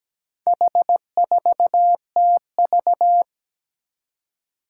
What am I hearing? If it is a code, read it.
Morse code, heard as a keyed tone, about 17 wpm, 700 Hz. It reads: H4TV
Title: Four Challenging Letter-Number Combinations 17wpm